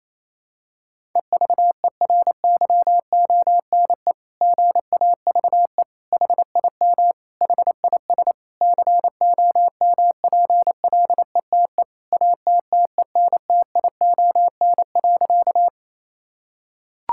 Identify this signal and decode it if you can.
Morse code, heard as a keyed tone, about 28 wpm, 700 Hz. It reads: EVERYONE GAVE HIM HIS COMPLETE ATTENTION.
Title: Sentences from Top 1000 Words - Review Part 6 - 28wpm